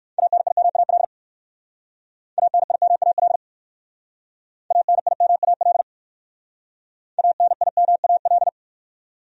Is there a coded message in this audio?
Yes: Morse code from 0.2 to 8.5 s, heard as a keyed tone, about 58 wpm, 700 Hz. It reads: ANIMAL ANIMAL ANIMAL ANIMAL